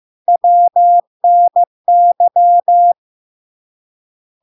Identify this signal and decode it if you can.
Morse code, heard as a keyed tone, about 15 wpm, 700 Hz. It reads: WNY
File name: LARK-FD-2022-Log-in-CW-15wpm-700-Hz